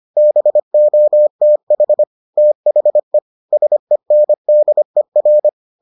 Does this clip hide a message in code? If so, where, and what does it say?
Yes: Morse code from 0.2 to 5.5 s, heard as a keyed tone, about 25 wpm, 600 Hz. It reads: BOTH THE SENDER